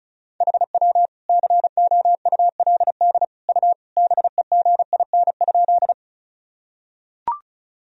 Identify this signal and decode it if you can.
Morse code, heard as a keyed tone, about 35 wpm, 700 Hz. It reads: HW COULD U BEGIN?
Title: Sentences from Top 300 Words and Abbreviations 35wpm